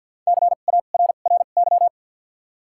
Morse code, heard as a keyed tone, about 53 wpm, spaced as for 37 wpm, 700 Hz. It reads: CARRY
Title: 2x Character Spacing - Top 500 Words - 50wpm